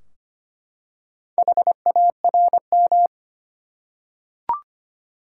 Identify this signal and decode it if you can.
Morse code, heard as a keyed tone, about 25 wpm, 700 Hz. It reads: HARM
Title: Four Letter Words - 25wpm